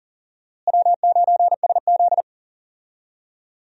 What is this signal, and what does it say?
Morse code, heard as a keyed tone, about 38 wpm, 700 Hz. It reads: W9SZ
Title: USA Call Signs - 4 Characters - Repeated 3x - 40wpm